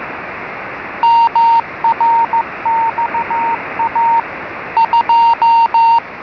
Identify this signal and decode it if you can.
Morse code, heard as a keyed tone, about 15 wpm, 920 Hz. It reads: MRXA 2